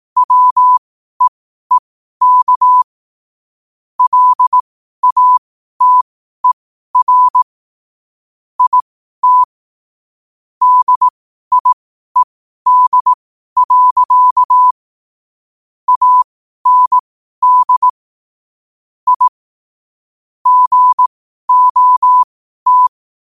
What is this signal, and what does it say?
Morse code, heard as a keyed tone, about 18 wpm, spaced as for 12 wpm, 1000 Hz. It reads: WEEK LATER IT DIED. AND I GOT
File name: qotdfiles / 12wpm